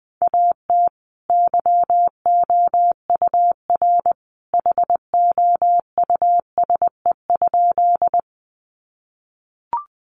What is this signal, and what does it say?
Morse code, heard as a keyed tone, about 20 wpm, 700 Hz. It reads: AT YOUR HOUSE?
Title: Sentences from Top 200 Words 20wpm